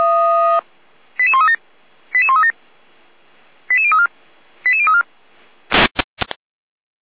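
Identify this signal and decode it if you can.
Five-tone selective call (ZVEI), heard as quick runs of five tones, tones of about 70 ms. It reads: 88137 88137 88034 88034
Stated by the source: ZVEI five-tone call received on SDR, code 88549.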